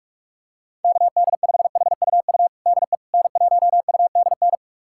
Morse code, heard as a keyed tone, about 46 wpm, 700 Hz. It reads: KD5HUU DE N1UDN
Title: USA Callsign DE Callsign 45wpm